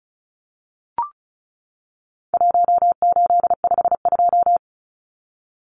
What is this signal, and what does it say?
Morse code, heard as a keyed tone, about 34 wpm, 700 Hz. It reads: 1852
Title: Numbers - 4-Digit 35wpm